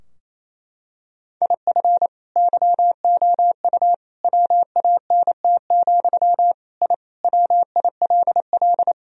Morse code, heard as a keyed tone, about 28 wpm, 700 Hz. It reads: IF YOU WANT, I WILL
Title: Sentences from Top 1000 Words - Review Part 3 - 28wpm